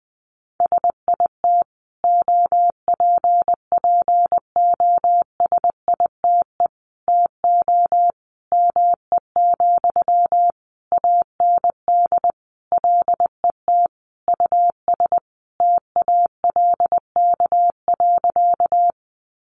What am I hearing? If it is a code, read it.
Morse code, heard as a keyed tone, about 19 wpm, 700 Hz. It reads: SIT OPPOSITE TO ME, AND LET US TALK.
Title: Sentences from Top 1300 Words 20wpm